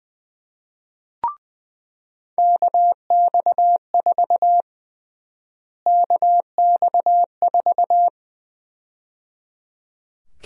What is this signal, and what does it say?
Morse code, heard as a keyed tone, about 20 wpm, 700 Hz. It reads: KX4 KX4
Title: USA Call Sign Prefixes - Repeated 2x - 20wpm